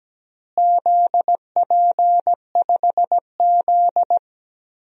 Morse code, heard as a keyed tone, about 17 wpm, 700 Hz. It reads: ZP5Z
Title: DXCC-100 Call Signs - 4 Characters - Repeated 2x - 17wpm